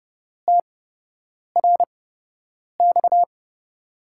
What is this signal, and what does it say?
Morse code, heard as a keyed tone, about 29 wpm, spaced as for 8 wpm, 700 Hz. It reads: TRX
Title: USA Call Signs - 8x Character Spacing - 30wpm